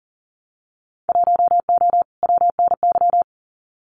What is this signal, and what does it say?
Morse code, heard as a keyed tone, about 37 wpm, 700 Hz. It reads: 1O WNY
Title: Winter Field Day Exchanges 40wpm